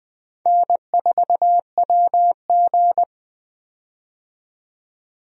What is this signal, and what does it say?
Morse code, heard as a keyed tone, about 20 wpm, 700 Hz. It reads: N4WG
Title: USA Call Signs - 4 Characters - 20wpm